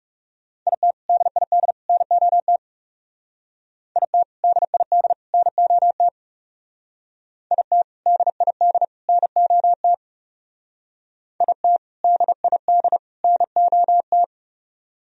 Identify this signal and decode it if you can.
Morse code, heard as a keyed tone, about 38 wpm, 690 Hz. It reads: IT DID NOT IT DID NOT IT DID NOT IT DID NOT